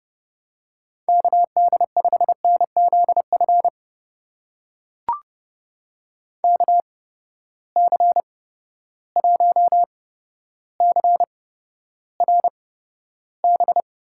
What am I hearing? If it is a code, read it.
Morse code, heard as a keyed tone, about 32 wpm, spaced as for 13 wpm, 700 Hz. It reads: KD5NZF KC1CRB
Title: USA Call Signs - 6 Characters - 8x Character Spacing - 30wpm